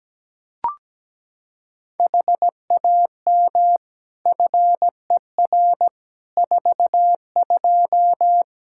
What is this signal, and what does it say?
Morse code, heard as a keyed tone, about 17 wpm, 700 Hz. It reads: HAM FER 42
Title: Practice QSO - Exchange 3 - More Info - 17wpm